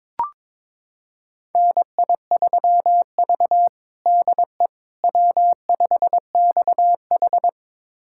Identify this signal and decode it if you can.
Morse code, heard as a keyed tone, about 22 wpm, 700 Hz. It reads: NI3V DE W5XH
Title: USA Callsign DE Callsign - 4 Characters - 22wpm